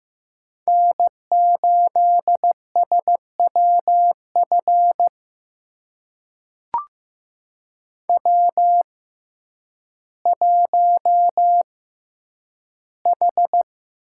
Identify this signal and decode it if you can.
Morse code, heard as a keyed tone, about 15 wpm, spaced as for 9 wpm, 700 Hz. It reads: N8SWF W1H
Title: USA Call Signs - 5 Characters - 6x Character Spacing - 15wpm